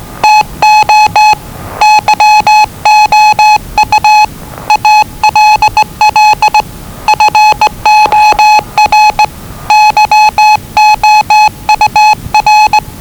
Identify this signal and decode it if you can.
Morse code, heard as a keyed tone, about 20 wpm, 850 Hz. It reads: TO YOU ALL FOR YOUR
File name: WellingtonRadioZlwCloseDownOnHfAndMf / 01-Zlw 4277khz 417.5 qrt until 0925z